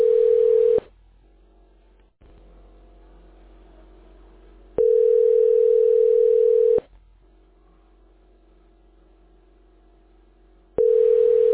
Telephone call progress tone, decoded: ringback tone

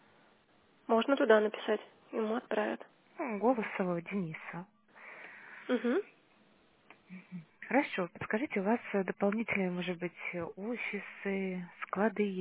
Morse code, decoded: none

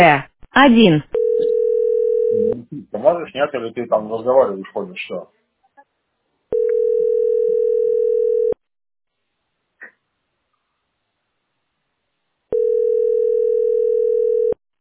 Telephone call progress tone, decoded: ringback tone

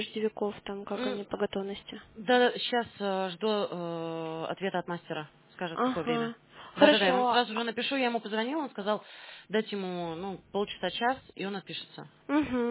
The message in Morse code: none

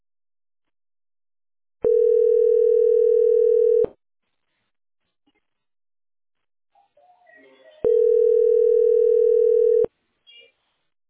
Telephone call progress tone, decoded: ringback tone

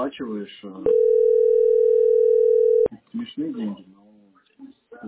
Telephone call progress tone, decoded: ringback tone

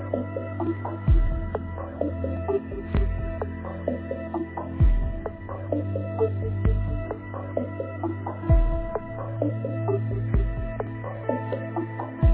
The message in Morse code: none

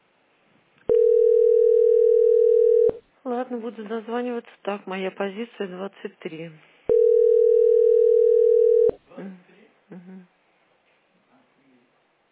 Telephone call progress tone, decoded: ringback tone

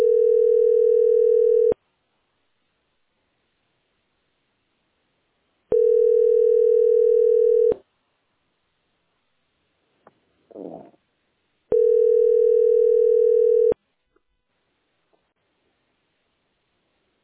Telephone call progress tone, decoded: ringback tone

